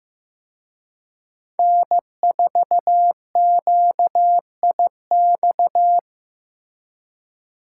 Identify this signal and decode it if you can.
Morse code, heard as a keyed tone, about 18 wpm, 700 Hz. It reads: N4QIX
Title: Speed Racing - USA 1x3 Call Signs 15wpm